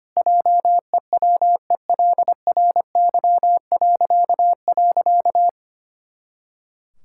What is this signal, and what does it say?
Morse code, heard as a keyed tone, about 25 wpm, 700 Hz. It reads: JEWELRY..
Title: Pangram Challenge v2 - 25wpm